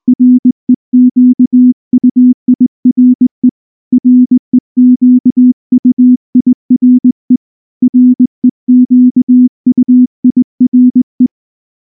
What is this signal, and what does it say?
Morse code, heard as a keyed tone, about 19 wpm, 250 Hz. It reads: REQUIRE REQUIRE REQUIRE
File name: morse-training-20WPM